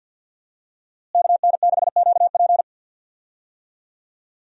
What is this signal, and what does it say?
Morse code, heard as a keyed tone, about 46 wpm, 700 Hz. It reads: KN6QP
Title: USA Call Signs - 5 Characters - Repeated 3x - 50wpm